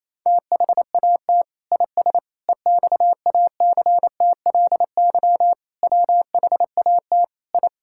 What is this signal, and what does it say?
Morse code, heard as a keyed tone, about 27 wpm, 700 Hz. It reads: THAT IS EXACTLY WHAT I
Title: Sentences from Top 1000 Words - Review Part 6 - 28wpm